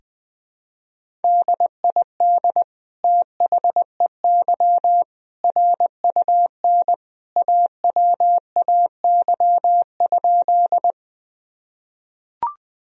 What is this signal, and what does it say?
Morse code, heard as a keyed tone, about 20 wpm, 700 Hz. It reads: DID THEY RUN AWAY?
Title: Sentences from Top 1000 Words - Review Part 1 - 20wpm